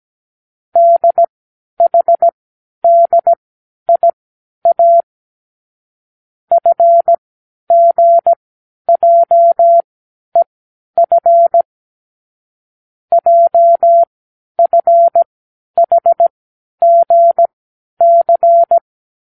Morse code, heard as a keyed tone, about 17 wpm, spaced as for 10 wpm, 680 Hz. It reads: DHDIA FGJEF JFHGC
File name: k7qo / Lesson15(A-J)